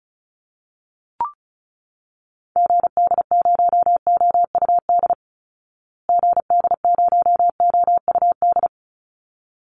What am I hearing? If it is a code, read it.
Morse code, heard as a keyed tone, about 34 wpm, 700 Hz. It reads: GD0OUD GD0OUD
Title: Call Signs Repeated 3x - 35wpm